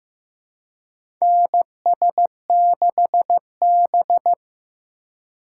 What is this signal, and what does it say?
Morse code, heard as a keyed tone, about 15 wpm, 700 Hz. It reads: NS6B
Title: USA Call Signs - 4 Characters - 4x Character Spacing - 15wpm